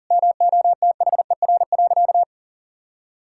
Morse code, heard as a keyed tone, about 40 wpm, 700 Hz. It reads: MOTHER.
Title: Sentences from Top 1000 Words - Review Part 5 - 40wpm